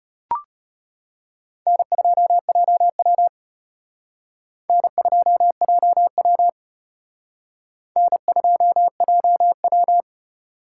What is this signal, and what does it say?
Morse code, heard as a keyed tone, about 34 wpm, 700 Hz. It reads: N2JW N2JW N2JW